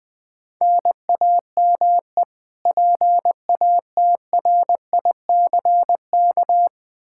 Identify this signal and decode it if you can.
Morse code, heard as a keyed tone, about 21 wpm, 700 Hz. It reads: NAME PATRICK
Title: Practice QSO - Exchange 1 - Beginning - 20wpm